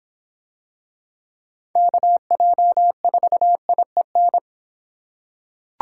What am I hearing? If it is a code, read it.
Morse code, heard as a keyed tone, about 25 wpm, 700 Hz. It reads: KJ4IEN